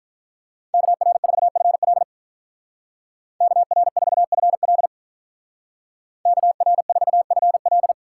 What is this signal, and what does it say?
Morse code, heard as a keyed tone, about 42 wpm, 700 Hz. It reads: KR4FL KR4FL KR4FL
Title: Speed Racing - USA 2x2 Call Signs 35wpm